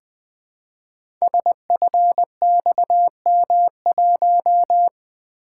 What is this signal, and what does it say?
Morse code, heard as a keyed tone, about 20 wpm, 700 Hz. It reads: SFXM1
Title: Code Groups - 5 Letter:Numbers - 20wpm